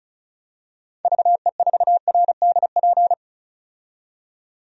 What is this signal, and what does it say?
Morse code, heard as a keyed tone, about 36 wpm, 700 Hz. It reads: VE4RDP
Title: DXCC-100 Call Signs - 6 Characters - 35wpm